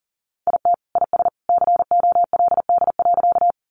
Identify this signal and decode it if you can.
Morse code, heard as a keyed tone, about 38 wpm, 700 Hz. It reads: IT IS COLD.
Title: Sentences from Top 400 Words 40wpm